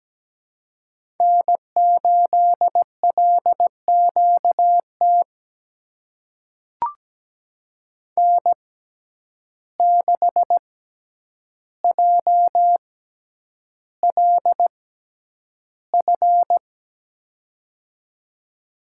Morse code, heard as a keyed tone, about 17 wpm, spaced as for 9 wpm, 700 Hz. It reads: N8LQT N6JLF
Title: USA Call Signs - 5 Characters - 6x Character Spacing - 17wpm